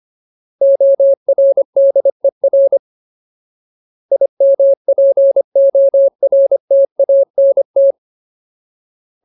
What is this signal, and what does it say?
Morse code, heard as a keyed tone, about 25 wpm, 550 Hz. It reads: ORDER IMPORTANT
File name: Ultimate Word List / 25WPM Ultimate